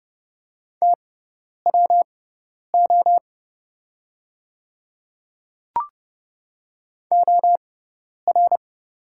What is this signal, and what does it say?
Morse code, heard as a keyed tone, about 30 wpm, spaced as for 10 wpm, 690 Hz. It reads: TWO OR
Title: 6x Character Spacing - Top 100 Words - 30wpm